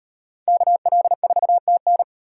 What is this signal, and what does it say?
Morse code, heard as a keyed tone, about 38 wpm, 700 Hz. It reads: KL4TN